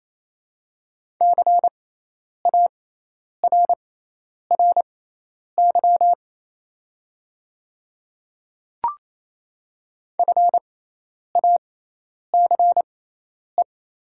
Morse code, heard as a keyed tone, about 28 wpm, spaced as for 10 wpm, 700 Hz. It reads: CARRY FACE
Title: 6x Character Spacing - Top 300 Words - 28wpm